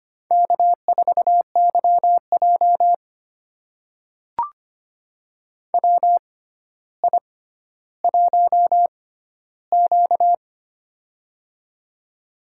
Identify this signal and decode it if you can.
Morse code, heard as a keyed tone, about 25 wpm, spaced as for 13 wpm, 700 Hz. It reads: K4YJ WI1Q
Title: USA Call Signs - 4 Characters - 6x Character Spacing - 25wpm